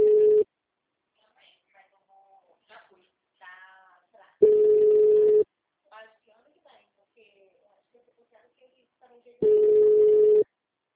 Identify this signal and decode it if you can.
Ringback tone (Europe, 425 Hz), with speech.